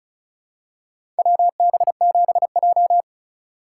Morse code, heard as a keyed tone, about 34 wpm, 700 Hz. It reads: WB7J
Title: USA Call Signs - 4 Characters - 3x Character Spacing - 35wpm